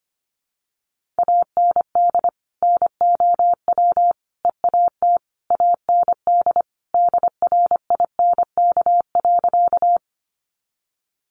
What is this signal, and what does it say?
Morse code, heard as a keyed tone, about 25 wpm, 700 Hz. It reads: AND NOW EAT AND DRINK.
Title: Sentences from Top 1500 Words 25wpm